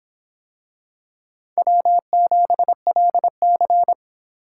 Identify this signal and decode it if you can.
Morse code, heard as a keyed tone, about 24 wpm, 700 Hz. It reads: W7LC